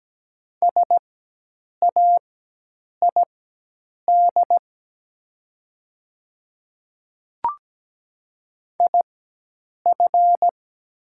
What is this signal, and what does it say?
Morse code, heard as a keyed tone, about 17 wpm, spaced as for 8 wpm, 700 Hz. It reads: SAID IF
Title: 4x Character Spacing - Top 100 Words - 17wpm